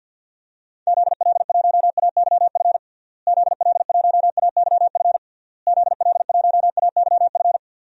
Morse code, heard as a keyed tone, about 52 wpm, 700 Hz. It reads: ZL1AYF ZL1AYF ZL1AYF
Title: DXCC-100 Call Signs - 6 Characters - Repeated 3x - 50wpm